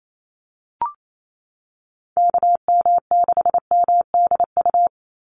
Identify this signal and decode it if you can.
Morse code, heard as a keyed tone, about 28 wpm, 700 Hz. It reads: KM6MDU
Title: USA Call Signs 28wpm